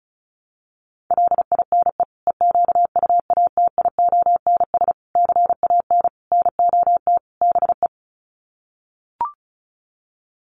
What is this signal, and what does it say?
Morse code, heard as a keyed tone, about 34 wpm, 700 Hz. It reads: LINE EQUATIONS CAN NOT BE
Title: Sets of 5 Words 35wpm